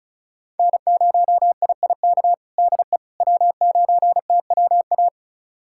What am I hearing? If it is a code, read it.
Morse code, heard as a keyed tone, about 34 wpm, 700 Hz. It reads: N0IIK DE W9TWA
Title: USA Callsign DE Callsign - 5 Characters - 35wpm